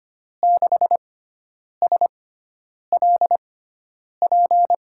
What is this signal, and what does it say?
Morse code, heard as a keyed tone, about 25 wpm, spaced as for 9 wpm, 700 Hz. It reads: 6SLP
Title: USA Call Signs - 6 Characters - 6x Character Spacing - 25wpm